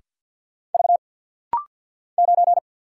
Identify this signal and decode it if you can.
Morse code, heard as a keyed tone, about 50 wpm, 690 Hz. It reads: V 9
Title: Single Letter-Number Alt - Rapid-Fire 50wpm